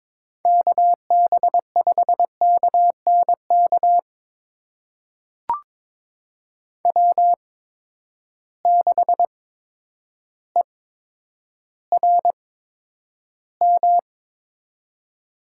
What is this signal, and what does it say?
Morse code, heard as a keyed tone, about 22 wpm, spaced as for 10 wpm, 700 Hz. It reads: KB5KNK W6ERM
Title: USA Call Signs - 8x Character Spacing - 22wpm